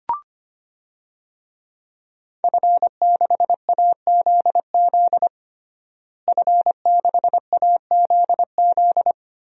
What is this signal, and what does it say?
Morse code, heard as a keyed tone, about 25 wpm, 700 Hz. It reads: F6AZZ F6AZZ
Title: DXCC-100 Call Signs - 5 Characters - Repeated 3x - 25wpm